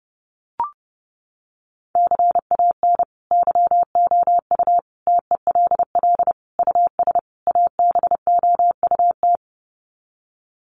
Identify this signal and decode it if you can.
Morse code, heard as a keyed tone, about 30 wpm, 700 Hz. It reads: CAN YOU TELL US ABOUT
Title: Sets of 5 Words Encore2 30wpm